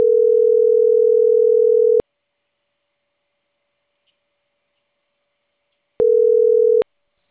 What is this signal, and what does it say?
Ringback tone (North America), 440 + 480 Hz.